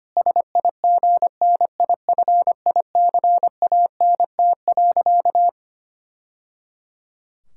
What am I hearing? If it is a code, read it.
Morse code, heard as a keyed tone, about 25 wpm, 700 Hz. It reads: SIGNIFICANT.
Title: Sentences from Top 1000 Words - Review Part 6 - 25wpm